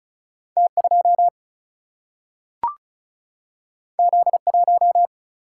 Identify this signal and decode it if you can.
Morse code, heard as a keyed tone, about 36 wpm, 700 Hz. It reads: T2 Z1
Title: Single-Letter Single-Number 35wpm